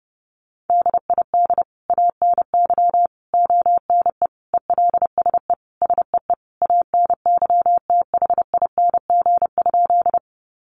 Morse code, heard as a keyed tone, about 30 wpm, 700 Hz. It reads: DID ANY ONE ELSE SEE ANYTHING?
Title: Sentences from Top 1000 Words 30wpm